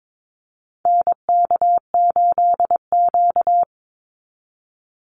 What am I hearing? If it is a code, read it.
Morse code, heard as a keyed tone, about 22 wpm, 700 Hz. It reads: NK8Q
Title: NA SOTA Activator Callsigns 22wpm